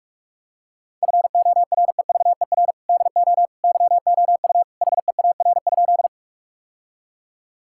Morse code, heard as a keyed tone, about 46 wpm, 700 Hz. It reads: FOREVER DO YOU HEAR?
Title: Sentences from Top 1600 Words 45wpm